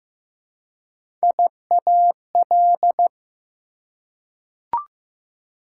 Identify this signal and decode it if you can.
Morse code, heard as a keyed tone, about 15 wpm, 700 Hz. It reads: IAL
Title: Prefixes and Suffixes 15wpm